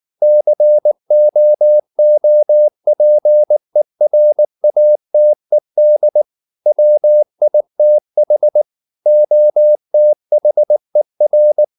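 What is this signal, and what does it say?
Morse code, heard as a keyed tone, about 19 wpm, 600 Hz. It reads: COOPERATED WITH OTHER
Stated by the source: Morse code at 19 wpm from CW med Gnister, reading Ken Follett's novel Never in English.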